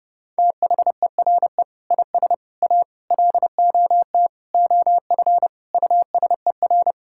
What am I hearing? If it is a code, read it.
Morse code, heard as a keyed tone, about 31 wpm, 700 Hz. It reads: THERE IS A LOT OF USER
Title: Sentences from Top 1000 Words - Review Part 6 - 30wpm